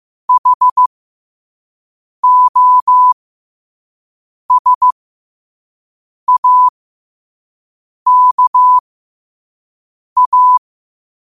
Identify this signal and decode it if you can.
Morse code, heard as a keyed tone, about 15 wpm, spaced as for 5 wpm, 1000 Hz. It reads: HOSAKA
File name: qotdfiles / 5wpm